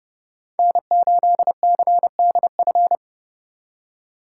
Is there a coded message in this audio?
Yes: Morse code from 0.6 to 3.0 s, heard as a keyed tone, about 28 wpm, 700 Hz. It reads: N8CDF